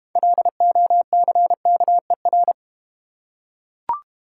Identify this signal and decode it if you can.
Morse code, heard as a keyed tone, about 31 wpm, 700 Hz. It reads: LOCKER
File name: Easy Words - Speed-Racing with Hints - 25wpm